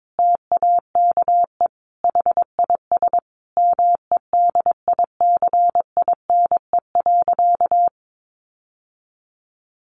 Morse code, heard as a keyed tone, about 22 wpm, 700 Hz. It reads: TAKE HIS MEDICINE.
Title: Sentences from Top 900 Words 22wpm